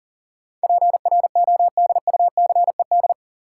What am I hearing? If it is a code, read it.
Morse code, heard as a keyed tone, about 37 wpm, 700 Hz. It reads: PRODUCED
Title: Sentences from Top 1000 Words - Review Part 6 - New Word Drill - 40wpm